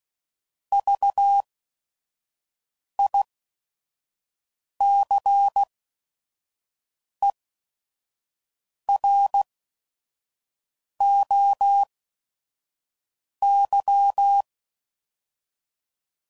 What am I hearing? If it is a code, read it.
Morse code, heard as a keyed tone, about 16 wpm, spaced as for 5 wpm, 800 Hz. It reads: VICEROY
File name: Prefixes / By First Letter / Vs